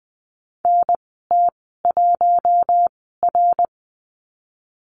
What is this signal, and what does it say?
Morse code, heard as a keyed tone, about 20 wpm, spaced as for 14 wpm, 700 Hz. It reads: NT1R
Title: 055 Lesson 10g - Letter r - Call Signs 20wpm